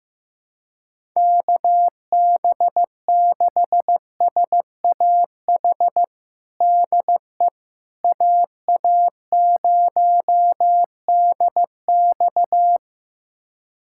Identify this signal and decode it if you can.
Morse code, heard as a keyed tone, about 15 wpm, 700 Hz. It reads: KB6SAH DE AA0DX
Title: USA Callsign DE Callsign 15wpm